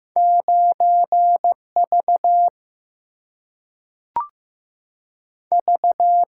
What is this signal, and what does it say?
Morse code, heard as a keyed tone, about 15 wpm, 700 Hz. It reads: 9V V
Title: Single-Letter Single-Number 15wpm